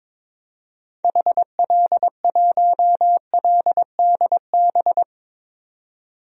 Morse code, heard as a keyed tone, about 22 wpm, 700 Hz. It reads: HL1LDB
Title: DXCC-100 Call Signs - 6 Characters - 22wpm